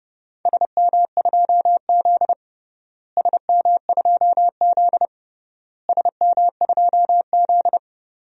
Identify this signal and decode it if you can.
Morse code, heard as a keyed tone, about 30 wpm, 700 Hz. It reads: SM2Z SM2Z SM2Z
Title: DXCC-100 Call Signs - 4 Characters - Repeated 3x - 30wpm